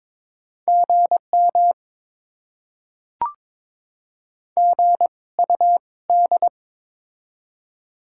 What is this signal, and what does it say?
Morse code, heard as a keyed tone, about 22 wpm, spaced as for 18 wpm, 690 Hz. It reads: GM GUD
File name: Top 20 QSO Elements - 2x Character Spacing - 22wpm